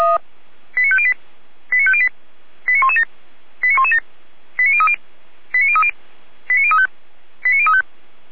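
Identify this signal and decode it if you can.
Five-tone selective call (ZVEI), heard as quick runs of five tones, tones of about 70 ms. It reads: 89508 89508 89117 89117 89030 89030 89035 89035
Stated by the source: ZVEI five-tone selective call received by SDR.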